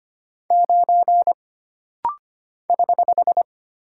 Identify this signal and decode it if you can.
Morse code, heard as a keyed tone, about 25 wpm, 700 Hz. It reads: 9 <HH>
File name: Single Character (Letter:Number:Punctuation:Prosign) - Rapid-Fire - 25wpm